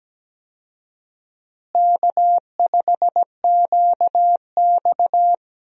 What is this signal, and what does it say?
Morse code, heard as a keyed tone, about 17 wpm, 700 Hz. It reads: K5QX
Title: USA Call Signs - 4 Characters - Repeated 3x - 17wpm